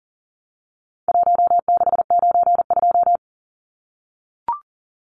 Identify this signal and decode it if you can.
Morse code, heard as a keyed tone, about 40 wpm, 700 Hz. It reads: 1692
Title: Numbers - 4-Digit 40wpm